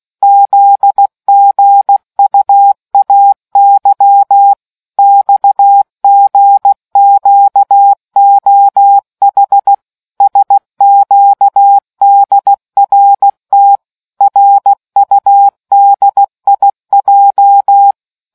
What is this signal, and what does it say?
Morse code, heard as a keyed tone, about 16 wpm, 800 Hz. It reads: ZGUAY XGQOH SQDRT RUDIJ